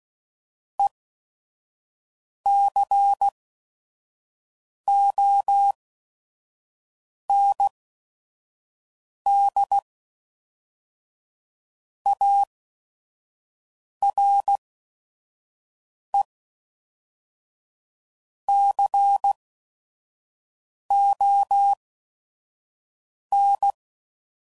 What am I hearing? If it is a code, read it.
Morse code, heard as a keyed tone, about 16 wpm, 800 Hz. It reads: E C O N D A R E C O N